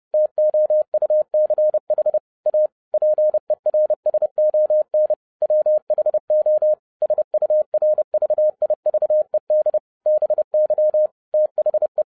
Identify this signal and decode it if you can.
Morse code, heard as a keyed tone, about 30 wpm, 610 Hz. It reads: TOUCH A PERSON WHO SURVIVED BY THE
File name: 2011-07-mp3 / n60 BACK FROM A BIG ACCIDENT 30wpm